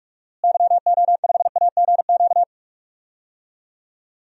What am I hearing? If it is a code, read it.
Morse code, heard as a keyed tone, about 46 wpm, 690 Hz. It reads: YO5AGQ
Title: DXCC-100 Call Signs - 6 Characters - Repeated 2x - 45wpm